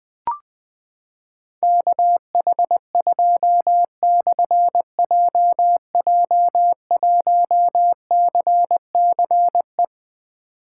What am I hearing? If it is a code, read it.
Morse code, heard as a keyed tone, about 20 wpm, 700 Hz. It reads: KH2/JJ1CCE
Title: Difficult Call Signs 20wpm